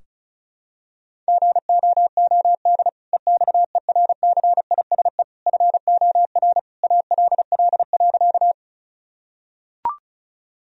Morse code, heard as a keyed tone, about 36 wpm, 700 Hz. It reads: GOOD EXERCISE FOR ALL.
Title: Sentences from Top 1000 Words - Review Part 4 - 35wpm